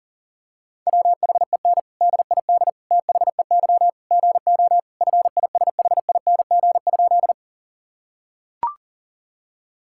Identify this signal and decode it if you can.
Morse code, heard as a keyed tone, about 40 wpm, 700 Hz. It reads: WHEN DID THEY GO FISHING?
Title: Sentences from Top 1000 Words - Review Part 5 - 40wpm